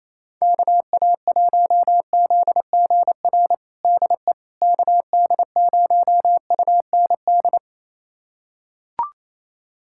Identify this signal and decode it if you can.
Morse code, heard as a keyed tone, about 27 wpm, 700 Hz. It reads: KA1ZGR DE KD0UND
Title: USA Callsign DE Callsign - 6 Characters - 28wpm